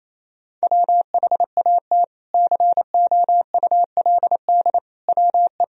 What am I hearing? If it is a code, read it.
Morse code, heard as a keyed tone, about 28 wpm, 700 Hz. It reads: WHAT COULD WE
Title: Sentences from Top 400 Words and Abbreviations 28wpm